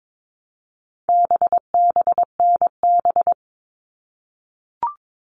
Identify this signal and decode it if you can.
Morse code, heard as a keyed tone, about 22 wpm, 700 Hz. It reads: BBNB
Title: All Cut Numbers - 4-Digit 22wpm